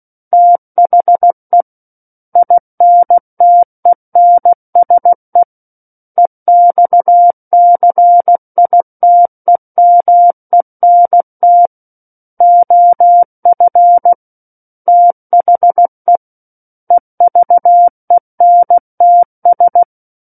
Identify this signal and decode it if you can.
Morse code, heard as a keyed tone, about 16 wpm, 700 Hz. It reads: THE INTENSE EXCITEMENT OF THE EVENTS
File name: Book One / 07-Chapter 7 - 16 WPM